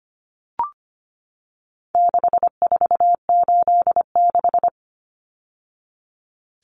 Morse code, heard as a keyed tone, about 25 wpm, 700 Hz. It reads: 6486
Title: Numbers - 4-Digit 25wpm